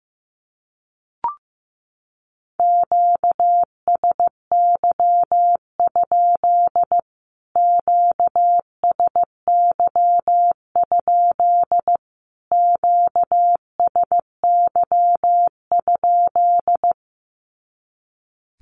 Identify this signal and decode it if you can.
Morse code, heard as a keyed tone, about 15 wpm, 700 Hz. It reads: QSY? QSY? QSY?
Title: Q-Codes Repeated 3x 15wpm